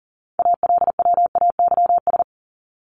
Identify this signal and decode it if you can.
Morse code, heard as a keyed tone, about 40 wpm, 700 Hz. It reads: ALWAYS
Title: Top 300 Words Repeated 40wpm